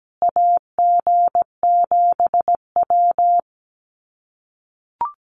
Morse code, heard as a keyed tone, about 17 wpm, 700 Hz. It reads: AG7W
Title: USA Call Signs Repeated 3x 17wpm